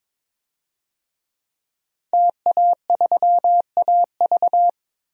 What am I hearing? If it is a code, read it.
Morse code, heard as a keyed tone, about 22 wpm, 700 Hz. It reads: TA3AV
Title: DXCC-100 Call Signs - 5 Characters - Repeated 2x - 22wpm